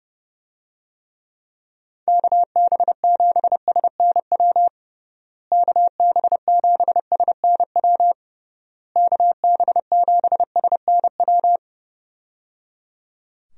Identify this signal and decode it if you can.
Morse code, heard as a keyed tone, about 30 wpm, 700 Hz. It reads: KB7SNW KB7SNW KB7SNW
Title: USA Call Signs - 6 Characters - Repeated 3x - 30wpm